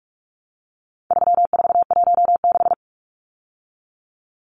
Morse code, heard as a keyed tone, about 44 wpm, 700 Hz. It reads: 3416
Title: Numbers - 4-Digit 45wpm